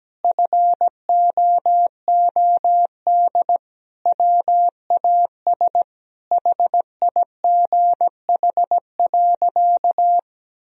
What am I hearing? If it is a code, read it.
Morse code, heard as a keyed tone, about 17 wpm, 700 Hz. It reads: FOOD WAS HIGH.
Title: Sentences from Top 1000 Words - Review Part 4 - 17wpm